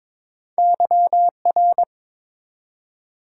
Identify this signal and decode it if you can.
Morse code, heard as a keyed tone, about 22 wpm, 700 Hz. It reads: YR
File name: practice / 22wpm / Two Letter-Number Combinations - 22wpm